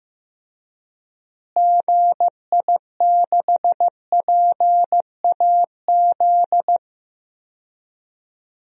Morse code, heard as a keyed tone, about 15 wpm, 700 Hz. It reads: GI6PAZ
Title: DXCC-100 Call Signs - 6 Characters - Repeated 3x - 15wpm